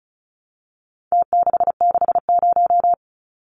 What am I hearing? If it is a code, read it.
Morse code, heard as a keyed tone, about 34 wpm, 700 Hz. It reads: T660
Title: Numbers - Contest Style - 4-Digit 35wpm